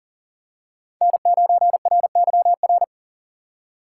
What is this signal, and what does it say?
Morse code, heard as a keyed tone, about 40 wpm, 700 Hz. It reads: N9RYR
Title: USA Call Signs - 4x Character Spacing - 40wpm